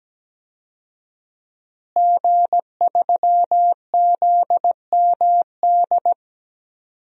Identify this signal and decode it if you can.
Morse code, heard as a keyed tone, about 17 wpm, 700 Hz. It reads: G3ZMD
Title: DXCC-100 Call Signs - 5 Characters - Repeated 2x - 17wpm